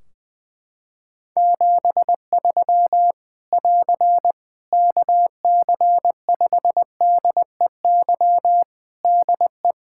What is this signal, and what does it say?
Morse code, heard as a keyed tone, about 20 wpm, 700 Hz. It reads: 73 <AR> KC5DEY DE
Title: Practice QSO - Exchange 4 - Ending - 20wpm